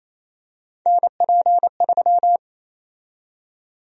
Morse code, heard as a keyed tone, about 28 wpm, 700 Hz. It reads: NP3
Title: USA Call Sign Prefixes - 28wpm